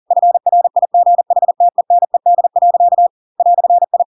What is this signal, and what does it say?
Morse code, heard as a keyed tone, about 40 wpm, 700 Hz. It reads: FRIGHTENED. "I
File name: Pinocchio / Pinocchio ch4 40 40 ik1hge